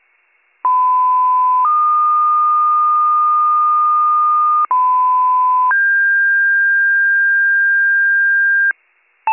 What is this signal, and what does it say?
Two-tone page, heard as a one-second tone, then a three-second tone, about 1010 then 1220 Hz, about 1010 then 1600 Hz, with noise. It is heard from 0.6 to 8.7 s.